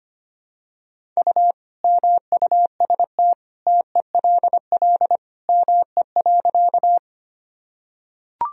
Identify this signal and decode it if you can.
Morse code, heard as a keyed tone, about 25 wpm, 700 Hz. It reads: U MUST TELL ME.
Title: Sentences from Top 200 Words and Abbreviations 25wpm